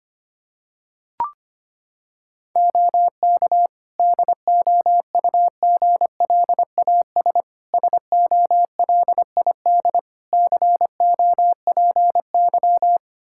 Morse code, heard as a keyed tone, about 25 wpm, 700 Hz. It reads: OK DOUGLAS SOLID COPY
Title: Practice QSO - Exchange 3 - More Info - 25wpm